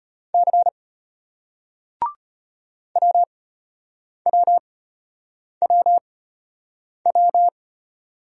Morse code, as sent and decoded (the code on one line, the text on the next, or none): -.-. / .-- .-- .-- .--
C WWWW